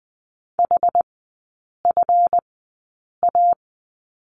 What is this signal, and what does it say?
Morse code, heard as a keyed tone, about 20 wpm, spaced as for 8 wpm, 700 Hz. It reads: HFA
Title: 152 Lesson 25e - Letter p - Three Characters 20wpm